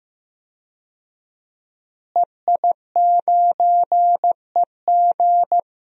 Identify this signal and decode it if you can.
Morse code, heard as a keyed tone, about 15 wpm, 700 Hz. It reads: EI9EG
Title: DXCC-100 Call Signs - 5 Characters - Repeated 3x - 15wpm